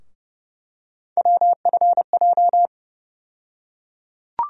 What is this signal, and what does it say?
Morse code, heard as a keyed tone, about 32 wpm, 700 Hz. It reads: WFJ
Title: USA Call Sign Postfixes - 3 Letters 30wpm